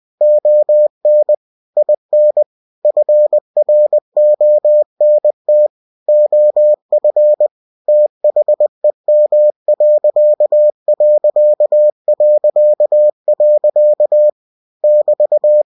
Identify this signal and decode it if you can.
Morse code, heard as a keyed tone, about 20 wpm, 600 Hz. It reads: ON IN FRONT OF THEM.... <BT>